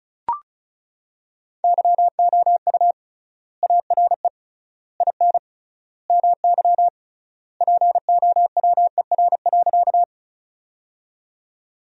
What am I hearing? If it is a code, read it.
Morse code, heard as a keyed tone, about 36 wpm, 700 Hz. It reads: YOU ARE IN MY POWER.
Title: 3x Word-Spacing - Sentences from Top 500 Words - 35wpm